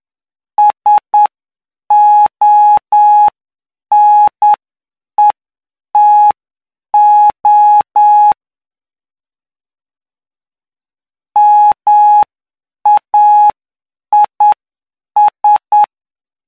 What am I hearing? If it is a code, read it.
Morse code, heard as a keyed tone, about 9 wpm, spaced as for 7 wpm, 840 Hz. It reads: SONETO MAIS
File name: cw / py2lcd / 06-PY2LCD-Li��o 15